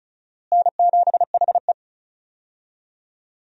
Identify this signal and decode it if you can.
Morse code, heard as a keyed tone, about 34 wpm, 700 Hz. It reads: N7HE
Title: USA Call Signs - 4 Characters - Repeated 3x - 35wpm